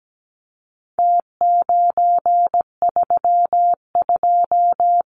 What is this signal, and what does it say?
Morse code, heard as a keyed tone, about 17 wpm, 700 Hz. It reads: T932
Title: Numbers - Contest Style - 4-Digit 17wpm